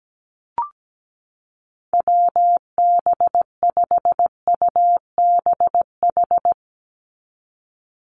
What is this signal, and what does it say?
Morse code, heard as a keyed tone, about 17 wpm, 700 Hz. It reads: WB5UBH
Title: USA Call Signs 17wpm